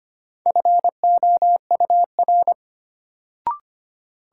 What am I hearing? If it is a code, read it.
Morse code, heard as a keyed tone, about 25 wpm, 700 Hz. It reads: FOUR
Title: Sentences from Top 300 Words - New Word Drill - 25wpm